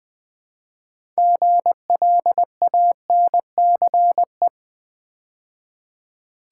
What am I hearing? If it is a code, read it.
Morse code, heard as a keyed tone, about 20 wpm, 700 Hz. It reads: GLANCE
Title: Build A Word - 3 Letters At A Time - 6-Letter Words - 20wpm